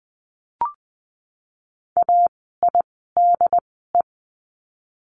Morse code, heard as a keyed tone, about 20 wpm, spaced as for 15 wpm, 700 Hz. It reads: AIDE
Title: 068 Lesson 12f - Letter d - Words 20wpm